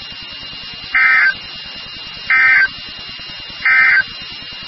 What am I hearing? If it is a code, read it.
Emergency Alert System (SAME) tones, heard as harsh buzzing bursts. It is heard from 0.8 to 4.1 s.
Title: NOAA Weather Radio WXK36